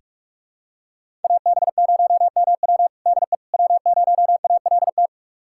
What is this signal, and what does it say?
Morse code, heard as a keyed tone, about 42 wpm, 700 Hz. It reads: AB0MW DE W0ALT
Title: USA Callsign DE Callsign - 5 Characters - 45wpm